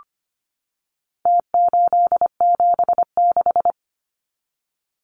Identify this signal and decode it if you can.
Morse code, heard as a keyed tone, about 25 wpm, 700 Hz. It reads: T876
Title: Numbers - Contest Style - 4-Digit 25wpm